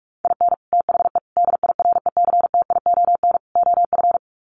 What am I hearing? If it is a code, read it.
Morse code, heard as a keyed tone, about 45 wpm, 700 Hz. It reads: IN THE DIRECTION OF
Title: Sets of 4 Words 45wpm